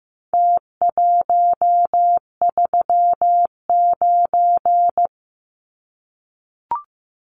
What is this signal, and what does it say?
Morse code, heard as a keyed tone, about 15 wpm, 700 Hz. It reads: T139
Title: Numbers - Contest Style - 4-Digit 15wpm